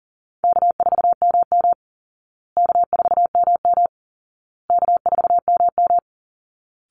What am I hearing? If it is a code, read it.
Morse code, heard as a keyed tone, about 40 wpm, 700 Hz. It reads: K4MM K4MM K4MM
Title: Easy Call Signs Repeated 3x - 40wpm